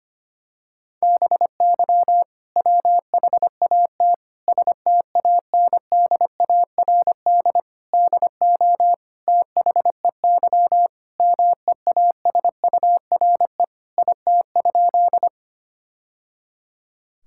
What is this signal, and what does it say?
Morse code, heard as a keyed tone, about 25 wpm, 700 Hz. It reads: BY WHAT STANDARD DO THEY MEASURE IT?
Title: Sentences from Top 1000 Words - Review Part 6 - 25wpm